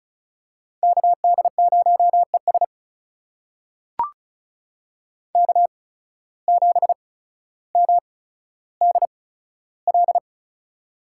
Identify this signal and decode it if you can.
Morse code, heard as a keyed tone, about 34 wpm, spaced as for 15 wpm, 700 Hz. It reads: KD0ES K7MDL
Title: USA Call Signs - 8x Character Spacing - 35wpm